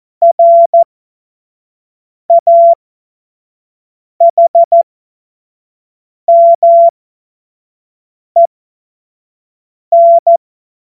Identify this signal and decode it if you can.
Morse code, heard as a keyed tone, about 14 wpm, spaced as for 5 wpm, 680 Hz. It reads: RAHMEN
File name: Morsekurs / Witze / Witz 12 Dezember 2023 25 F70